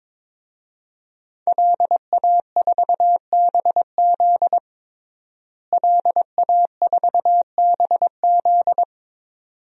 Morse code, heard as a keyed tone, about 22 wpm, 700 Hz. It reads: LA4BZ LA4BZ
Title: DXCC-100 Call Signs - 5 Characters - Repeated 2x - 22wpm